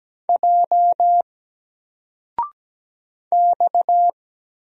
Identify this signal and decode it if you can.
Morse code, heard as a keyed tone, about 17 wpm, 700 Hz. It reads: J X
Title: Single Letter-Number 17wpm